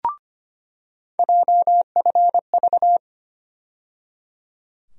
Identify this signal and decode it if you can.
Morse code, heard as a keyed tone, about 25 wpm, 700 Hz. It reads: JFV
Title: USA Call Sign Postfixes - 3 Letters 25wpm